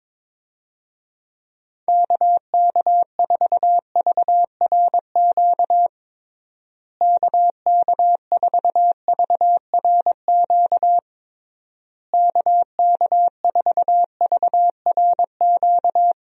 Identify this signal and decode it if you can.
Morse code, heard as a keyed tone, about 22 wpm, 700 Hz. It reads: KK4VRQ KK4VRQ KK4VRQ
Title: USA Call Signs - 6 Characters - Repeated 3x - 22wpm